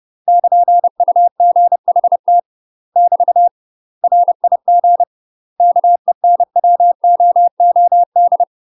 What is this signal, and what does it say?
Morse code, heard as a keyed tone, about 30 wpm, 700 Hz. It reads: <KN>UGHT <BT> RIG KENWOOD
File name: UK MASTER QSOs TRAINING / 30WPM / UK MASTER QSOs 0026